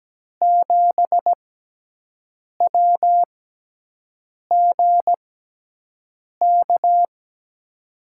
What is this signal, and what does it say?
Morse code, heard as a keyed tone, about 17 wpm, spaced as for 6 wpm, 700 Hz. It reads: 7WGK
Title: USA Call Signs - 5 Characters - 6x Character Spacing - 17wpm